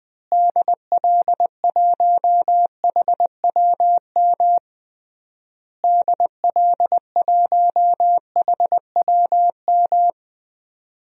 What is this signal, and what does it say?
Morse code, heard as a keyed tone, about 20 wpm, 700 Hz. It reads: DL1HWM DL1HWM
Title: DXCC-100 Call Signs - 6 Characters - Repeated 3x - 20wpm